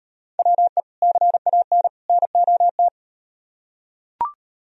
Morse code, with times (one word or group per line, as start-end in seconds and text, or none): 0.4-0.8 WE
1.0-1.9 CAN
2.1-2.9 NOT